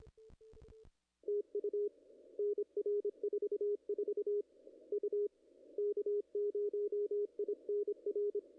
Morse code, heard as a keyed tone, about 26 wpm, 410 Hz. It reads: TU NR 44 U K0INR
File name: contests / ss-cw-k5tr-op-n5rz -0000